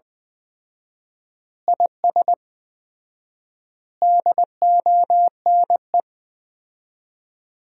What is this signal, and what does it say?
Morse code, heard as a keyed tone, about 20 wpm, 700 Hz. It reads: IS DONE
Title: 4x Word-Spacing - Sentences from Top 500 Words - 20wpm